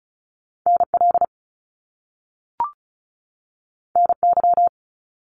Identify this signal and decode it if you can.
Morse code, heard as a keyed tone, about 36 wpm, 700 Hz. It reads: NL NY
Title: Top Two Letter Combinations in Words 35wpm